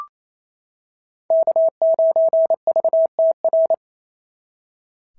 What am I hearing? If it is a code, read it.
Morse code, heard as a keyed tone, about 25 wpm, 650 Hz. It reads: K9VTR
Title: SOTA QSO Practice 28wpm